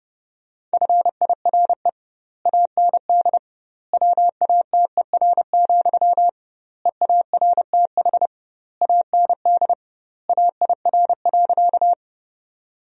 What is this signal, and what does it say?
Morse code, heard as a keyed tone, about 30 wpm, 700 Hz. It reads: FIRE AND WATER, EARTH AND AIR.
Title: 2x Word-Spacing - Sentences from Top 400 Words - 30wpm